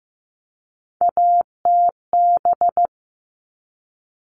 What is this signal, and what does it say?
Morse code, heard as a keyed tone, about 15 wpm, 700 Hz. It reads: ATB
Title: All Cut Numbers - 3-Digit 15wpm